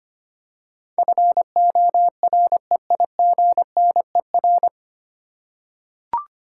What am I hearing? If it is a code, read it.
Morse code, heard as a keyed tone, about 25 wpm, 700 Hz. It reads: FOREIGNER
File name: practice / 25wpm / 9 Letter Words - First 3 Letters Given - 25wpm